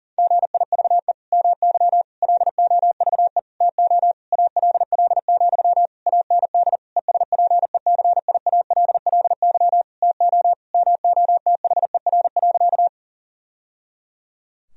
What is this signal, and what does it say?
Morse code, heard as a keyed tone, about 40 wpm, 700 Hz. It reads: GIVE MY LOVE TO ALL, AND ESPECIALLY TO MOTHER.
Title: Sentences from Top 1000 Words - Review Part 4 - 40wpm